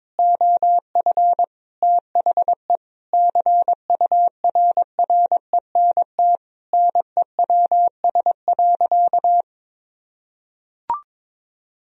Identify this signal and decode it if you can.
Morse code, heard as a keyed tone, about 22 wpm, 700 Hz. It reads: OF THE CURRENT NEWS.
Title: Sentences from Top 1000 Words - Review Part 4 - 22wpm